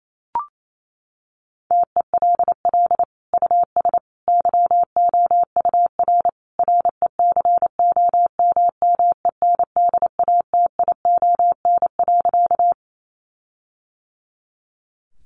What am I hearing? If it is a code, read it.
Morse code, heard as a keyed tone, about 28 wpm, 700 Hz. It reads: TELL US YOUR RECOMMENDATION.
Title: Sentences from Top 1400 Words 28wpm